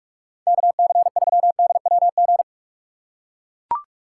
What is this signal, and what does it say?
Morse code, heard as a keyed tone, about 44 wpm, 700 Hz. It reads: KC3DWG
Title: USA Call Signs - 6 Characters - 6x Character Spacing - 45wpm